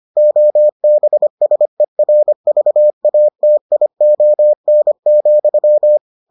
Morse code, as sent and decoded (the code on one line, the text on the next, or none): --- -... ... .-. ...- .- - .. --- -. --..--
OBSERVATION,